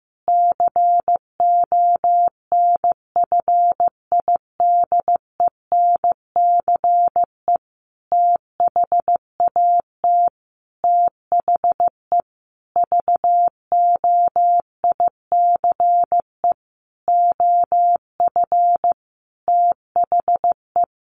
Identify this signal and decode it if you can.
Morse code, heard as a keyed tone, about 15 wpm, 700 Hz. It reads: CONFIDENCE THAT THE VOICE OF THE
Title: Sentences from Top 1500 Words 15wpm